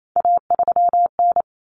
Morse code, heard as a keyed tone, about 28 wpm, 700 Hz. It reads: A3N
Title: Common Cut Numbers - 3-Digit 28wpm